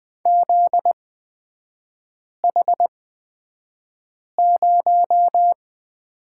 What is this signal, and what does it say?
Morse code, heard as a keyed tone, about 20 wpm, spaced as for 5 wpm, 700 Hz. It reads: ZH0
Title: 243 Lesson 39e - Letter z - Three Characters 20wpm